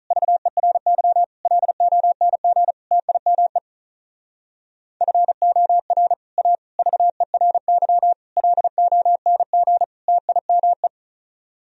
Morse code, heard as a keyed tone, about 46 wpm, 700 Hz. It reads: VERY LONG TIME FOR A VERY LONG TIME